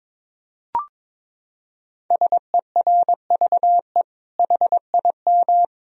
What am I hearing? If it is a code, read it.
Morse code, heard as a keyed tone, about 22 wpm, 700 Hz. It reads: SERVE HIM
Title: Sentences from Top 1000 Words - Review Part 2 - 22wpm